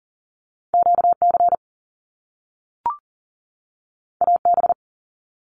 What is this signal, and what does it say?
Morse code, heard as a keyed tone, about 40 wpm, 700 Hz. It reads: QC AB
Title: Canadian Provinces and Territories 40wpm